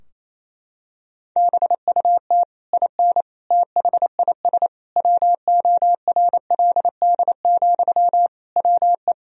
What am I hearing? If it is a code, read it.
Morse code, heard as a keyed tone, about 28 wpm, 700 Hz. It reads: BUT IN THIS WORLD, WE
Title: Sentences from Top 1000 Words - Review Part 4 - 28wpm